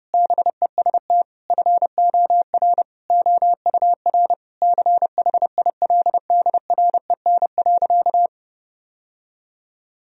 Morse code, heard as a keyed tone, about 30 wpm, 700 Hz. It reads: BEST FOR OUR CHILDREN.
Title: Sentences from Top 1000 Words - Review Part 4 - 30wpm